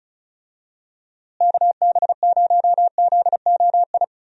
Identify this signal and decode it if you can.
Morse code, heard as a keyed tone, about 36 wpm, 700 Hz. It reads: KB0ZOI
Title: USA Call Signs - 6 Characters - Repeated 2x - 35wpm